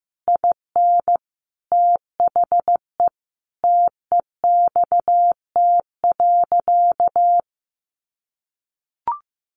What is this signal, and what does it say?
Morse code, heard as a keyed tone, about 15 wpm, 700 Hz. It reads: IN THE TEXT.
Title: Sentences from Top 1200 Words 15wpm